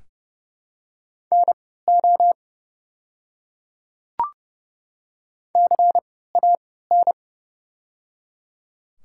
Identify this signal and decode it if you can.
Morse code, heard as a keyed tone, about 30 wpm, spaced as for 17 wpm, 700 Hz. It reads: NO CAN
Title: 3x Character Spacing - Top 100 Words - 30wpm